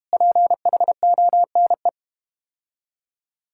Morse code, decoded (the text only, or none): PHONE